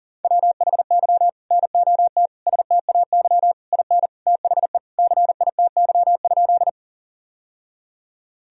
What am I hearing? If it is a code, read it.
Morse code, heard as a keyed tone, about 40 wpm, 700 Hz. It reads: WHY NOT STAY IN THE CITY?
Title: Sentences from Top 1000 Words - Review Part 1 - 40wpm